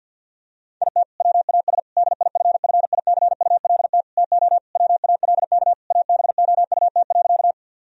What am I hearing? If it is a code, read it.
Morse code, heard as a keyed tone, about 49 wpm, 700 Hz. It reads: IT WAS DIFFICULT TO WALK ABOUT.
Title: Sentences from Top 1000 Words - Review Part 1 - 50wpm